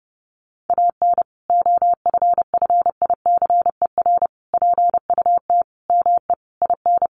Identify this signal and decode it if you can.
Morse code, heard as a keyed tone, about 30 wpm, 700 Hz. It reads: AN OFFICER PUT ME IN
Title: Sentences from Top 1200 Words 30wpm